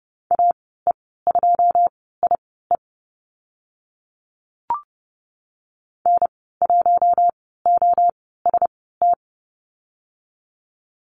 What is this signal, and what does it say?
Morse code, heard as a keyed tone, about 30 wpm, spaced as for 17 wpm, 700 Hz. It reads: AE2IE N1OST
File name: Lesson 19g - Letter w - Call Signs 30wpm